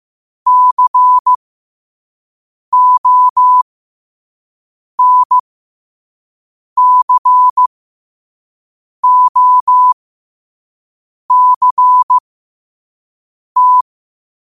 Morse code, heard as a keyed tone, about 15 wpm, spaced as for 5 wpm, 1000 Hz. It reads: CONCOCT
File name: qotdfiles / 5wpm